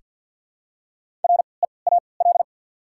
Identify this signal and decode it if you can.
Morse code, heard as a keyed tone, about 53 wpm, spaced as for 29 wpm, 700 Hz. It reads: REAL
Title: 3x Character Spacing - Top 500 Words - 50wpm